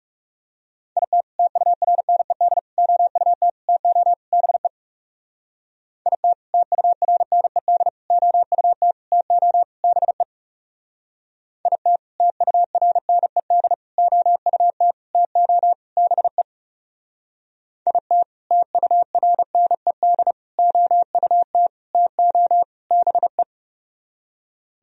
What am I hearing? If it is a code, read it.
Morse code, heard as a keyed tone, about 36 wpm, 700 Hz. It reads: IT TURNED OUT TO BE IT TURNED OUT TO BE IT TURNED OUT TO BE IT TURNED OUT TO BE
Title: Speed-Racing - Sets of 5 Words 30wpm